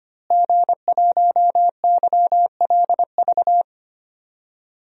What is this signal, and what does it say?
Morse code, heard as a keyed tone, about 25 wpm, 700 Hz. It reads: G1YLV
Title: DXCC-100 Call Signs - 5 Characters - Repeated 3x - 25wpm